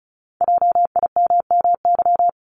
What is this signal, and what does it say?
Morse code, heard as a keyed tone, about 33 wpm, 690 Hz. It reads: JIMMY
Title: Multi-letter Phonemes 35wpm